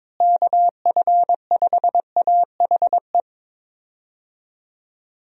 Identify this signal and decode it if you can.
Morse code, heard as a keyed tone, about 21 wpm, 700 Hz. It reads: KF5AHE
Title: USA Call Signs - 6 Characters - Repeated 2x - 22wpm